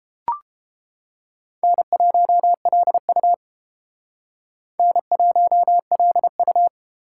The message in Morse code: N1LU N1LU

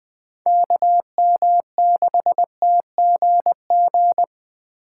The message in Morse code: KM6TGG

